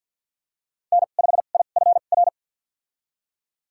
Morse code, heard as a keyed tone, about 52 wpm, spaced as for 36 wpm, 690 Hz. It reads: N5IFR
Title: USA Call Signs - 2x Character Spacing - 50wpm